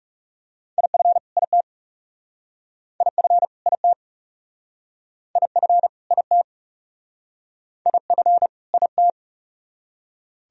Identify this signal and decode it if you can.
Morse code, heard as a keyed tone, about 41 wpm, 700 Hz. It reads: IF IT IF IT IF IT IF IT